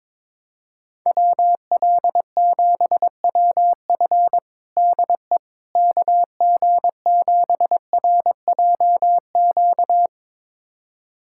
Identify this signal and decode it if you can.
Morse code, heard as a keyed tone, about 22 wpm, 700 Hz. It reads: WL7WF DE KG7RJQ
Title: USA Callsign DE Callsign 22wpm